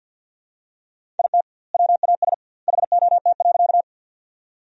Morse code, heard as a keyed tone, about 48 wpm, 700 Hz. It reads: IT WAS HOT.
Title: 2x Word-Spacing - Sentences from Top 500 Words - 50wpm